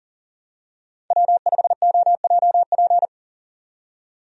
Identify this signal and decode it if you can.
Morse code, heard as a keyed tone, about 40 wpm, 700 Hz. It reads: W5OJP
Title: USA Call Signs - 5 Characters - 40wpm